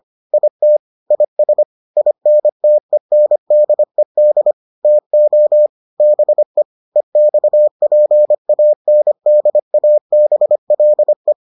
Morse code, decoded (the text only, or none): IT IS INTENDED TO BE EXPANDABLE